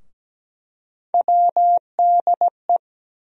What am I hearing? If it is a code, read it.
Morse code, heard as a keyed tone, about 17 wpm, 700 Hz. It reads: WDE
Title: USA Call Sign Postfixes - 3 Letters 17wpm